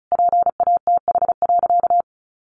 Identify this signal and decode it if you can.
Morse code, heard as a keyed tone, about 34 wpm, 700 Hz. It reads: PATH.
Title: Sentences from Top 1400 Words 35wpm